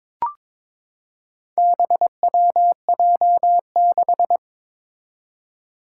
Morse code, heard as a keyed tone, about 22 wpm, 700 Hz. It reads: BWJ6
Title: Four Challenging Letter-Number Combinations 22wpm